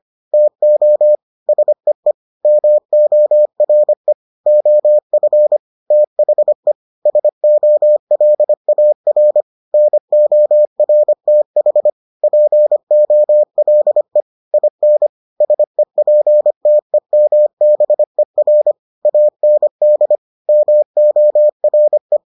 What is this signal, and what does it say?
Morse code, heard as a keyed tone, about 25 wpm, 600 Hz. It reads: TO SEE MORE OF THE SOLAR NORTH POLE IN SEPTEMBER AND MORE